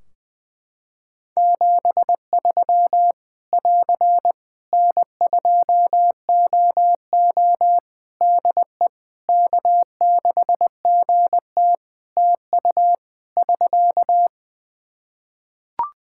Morse code, heard as a keyed tone, about 20 wpm, 700 Hz. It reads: 73 <AR> N2OO DE K6GT TU <SK>
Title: Practice QSO - Exchange 4 - Ending - 20wpm